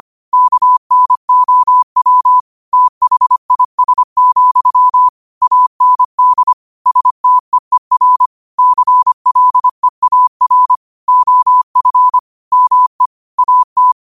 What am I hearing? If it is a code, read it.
Morse code, heard as a keyed tone, about 25 wpm, 1000 Hz. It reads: KNOW THIS, AND STEER CLEAR OF ME AT
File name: qotdfiles / 25wpm